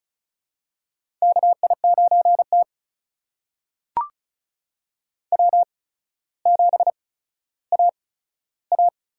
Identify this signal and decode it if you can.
Morse code, heard as a keyed tone, about 36 wpm, spaced as for 15 wpm, 700 Hz. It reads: KI9T W7AA
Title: USA Call Signs - 4 Characters - 8x Character Spacing - 35wpm